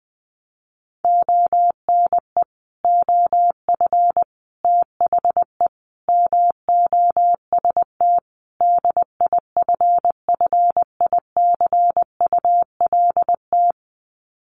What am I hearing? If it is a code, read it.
Morse code, heard as a keyed tone, about 20 wpm, 700 Hz. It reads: ONE OF THE MOST DIFFICULT
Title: Sets of 5 Words Encore2 20wpm